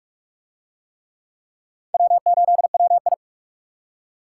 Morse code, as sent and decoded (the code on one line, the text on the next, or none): .-- ---.. .-- ..
W8WI